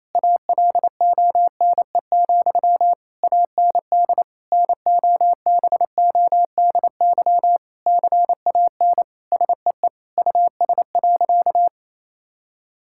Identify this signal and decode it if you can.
Morse code, heard as a keyed tone, about 28 wpm, 700 Hz. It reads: ALONE, AND NOBODY CAN SEE US.
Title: Sentences from Top 1600 Words 28wpm